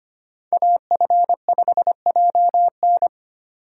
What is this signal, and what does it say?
Morse code, heard as a keyed tone, about 25 wpm, 700 Hz. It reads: AF5JN
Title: USA Call Signs - 5 Characters - 3x Character Spacing - 25wpm